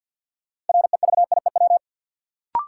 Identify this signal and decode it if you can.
Morse code, heard as a keyed tone, about 47 wpm, 700 Hz. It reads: REVIEW
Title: Sentences from Top 900 Words - New Word Drill - 50wpm